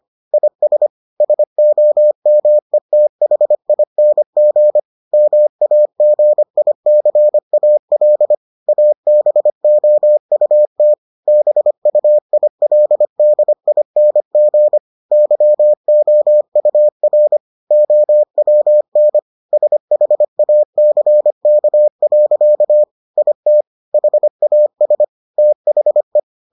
Morse code, heard as a keyed tone, about 25 wpm, 600 Hz. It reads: IS SOMETHING MAGICAL ABOUT BUILDING YOUR OWN SHACK. IT HAS THE